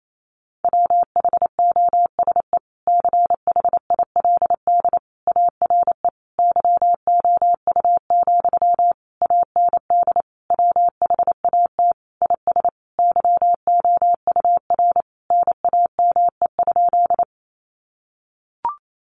Morse code, heard as a keyed tone, about 28 wpm, 700 Hz. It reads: WHOSE CHILD ARE YOU, AND WHAT IS YOUR NAME?
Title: Sentences from Top 1200 Words 28wpm